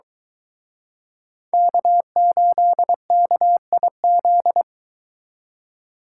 Morse code, heard as a keyed tone, about 23 wpm, 700 Hz. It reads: K8KIZ